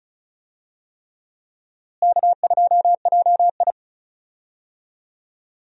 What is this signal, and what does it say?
Morse code, heard as a keyed tone, about 36 wpm, 700 Hz. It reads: K2JI
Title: USA Call Signs - 4 Characters - Repeated 3x - 35wpm